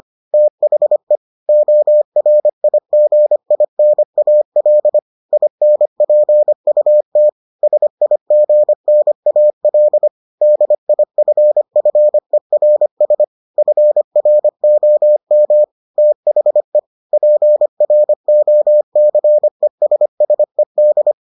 Morse code, heard as a keyed tone, about 25 wpm, 600 Hz. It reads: THE ORIGINAL INPUT SIGNAL DIFFERS FROM THE PROCESSED